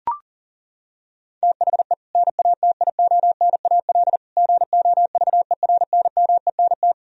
Morse code, heard as a keyed tone, about 40 wpm, 700 Hz. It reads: THE NATIONAL GOVERNMENT
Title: Sentences from Top 1000 Words - Review Part 4 - 40wpm